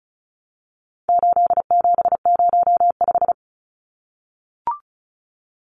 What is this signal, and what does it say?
Morse code, heard as a keyed tone, about 34 wpm, 700 Hz. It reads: 8705
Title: Numbers - 4-Digit 35wpm